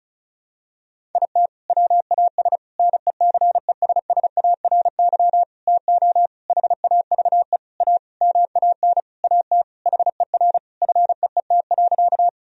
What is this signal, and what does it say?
Morse code, heard as a keyed tone, about 36 wpm, 700 Hz. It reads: IT WAS NECESSARY TO HAVE A MAN AT HER FEET.
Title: Sentences from Top 1000 Words - Review Part 1 - 35wpm